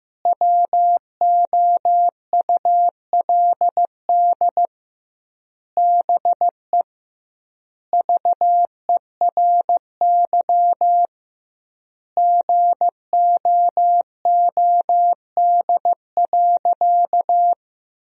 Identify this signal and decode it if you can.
Morse code, heard as a keyed tone, about 15 wpm, 700 Hz. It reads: WOULD BE VERY GOOD.
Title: 2x Word-Spacing - Sentences from Top 200 Words - 15wpm